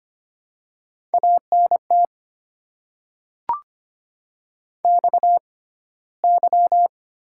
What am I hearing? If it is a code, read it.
Morse code, heard as a keyed tone, about 25 wpm, spaced as for 15 wpm, 700 Hz. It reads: ANT XY
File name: Top 90 QSO Elements - 6x Character Spacing - 25wpm